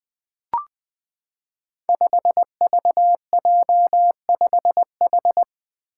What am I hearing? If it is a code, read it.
Morse code, heard as a keyed tone, about 20 wpm, 700 Hz. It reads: 5VJ5H
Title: Five Challenging Letter-Number Combinations 20wpm